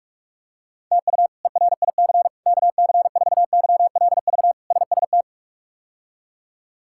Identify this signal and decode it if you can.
Morse code, heard as a keyed tone, about 46 wpm, 700 Hz. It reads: TU ERIC KC4YLV SST
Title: K1USN SST QSO Practice - 45wpm